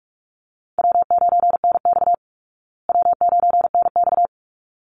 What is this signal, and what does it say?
Morse code, heard as a keyed tone, about 44 wpm, 700 Hz. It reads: W9NX W9NX
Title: Easy Call Signs Repeated 2x - 45wpm